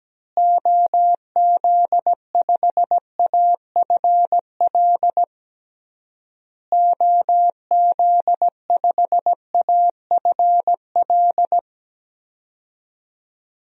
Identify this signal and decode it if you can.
Morse code, heard as a keyed tone, about 17 wpm, 700 Hz. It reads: OZ5AFL OZ5AFL
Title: DXCC-100 Call Signs - 6 Characters - Repeated 3x - 17wpm